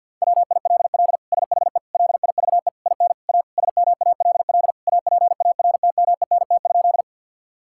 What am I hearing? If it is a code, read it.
Morse code, heard as a keyed tone, about 50 wpm, 700 Hz. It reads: WILL SHE LIVE IN A SMALL APARTMENT?
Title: Sentences from Top 1000 Words - Review Part 6 - 50wpm